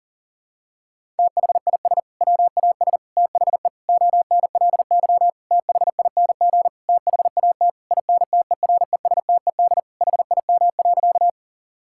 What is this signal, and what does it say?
Morse code, heard as a keyed tone, about 40 wpm, 700 Hz. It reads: THIS WAS THE ONLY THING THAT INTERESTED HIM.
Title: Sentences from Top 1000 Words - Review Part 6 - 40wpm